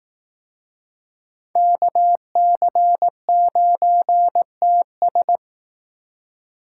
Morse code, heard as a keyed tone, about 19 wpm, 700 Hz. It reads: KC9TS
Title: Speed Racing - USA 2x2 Call Signs 15wpm